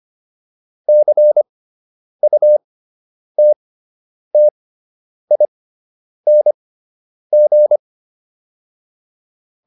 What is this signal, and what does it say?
Morse code, heard as a keyed tone, about 25 wpm, spaced as for 9 wpm, 600 Hz. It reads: CUTTING